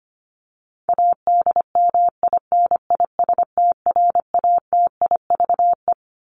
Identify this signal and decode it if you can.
Morse code, heard as a keyed tone, about 25 wpm, 700 Hz. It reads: ADMINISTRATIVE
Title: Top 1501-1750 Words 25wpm